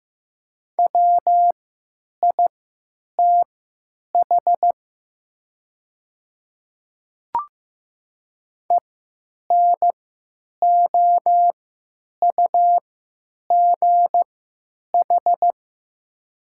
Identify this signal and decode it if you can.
Morse code, heard as a keyed tone, about 15 wpm, spaced as for 9 wpm, 700 Hz. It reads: WITH ENOUGH
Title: 3x Character Spacing - Top 500 Words - 15wpm